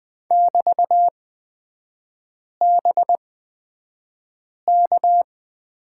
Morse code, heard as a keyed tone, about 20 wpm, spaced as for 5 wpm, 700 Hz. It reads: <BT>BK
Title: 224 Lesson 36e - BT - Three Characters 20wpm